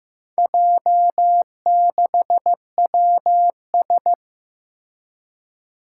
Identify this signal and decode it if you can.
Morse code, heard as a keyed tone, about 15 wpm, 700 Hz. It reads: J6WS
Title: Four Challenging Letter-Number Combinations 15wpm